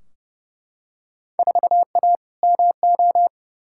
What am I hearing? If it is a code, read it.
Morse code, heard as a keyed tone, about 30 wpm, 700 Hz. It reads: 4A MO
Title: Field Day Exchanges - 30wpm